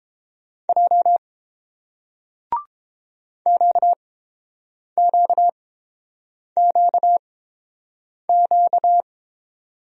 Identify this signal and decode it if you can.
Morse code, heard as a keyed tone, about 31 wpm, spaced as for 7 wpm, 700 Hz. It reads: J QQQQ